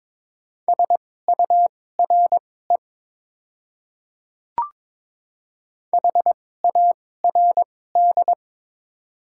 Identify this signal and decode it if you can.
Morse code, heard as a keyed tone, about 22 wpm, spaced as for 16 wpm, 700 Hz. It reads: SURE HARD
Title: 2x Character Spacing - Top 300 Words - 22wpm